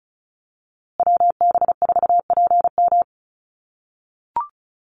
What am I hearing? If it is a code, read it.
Morse code, heard as a keyed tone, about 34 wpm, 700 Hz. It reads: WB4PM
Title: USA Call Signs Repeated 2x 35wpm